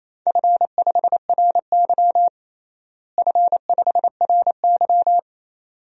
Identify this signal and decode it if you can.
Morse code, heard as a keyed tone, about 28 wpm, 700 Hz. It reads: F5RY F5RY
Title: DXCC-100 Call Signs - 4 Characters - Repeated 2x - 28wpm